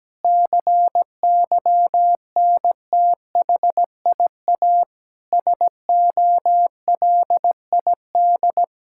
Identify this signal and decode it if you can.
Morse code, heard as a keyed tone, about 17 wpm, 700 Hz. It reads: CYNTHIA SOLID
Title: Practice QSO - Exchange 3 - More Info - 17wpm